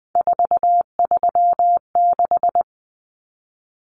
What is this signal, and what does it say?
Morse code, heard as a keyed tone, about 20 wpm, 700 Hz. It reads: 436
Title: Common Cut Numbers - 3-Digit 20wpm